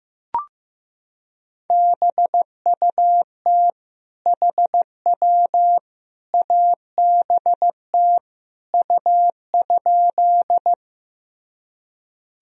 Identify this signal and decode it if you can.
Morse code, heard as a keyed tone, about 16 wpm, 700 Hz. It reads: BUT HW ABT U?
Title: Sentences from Top 100 Words and Abbreviations 15wpm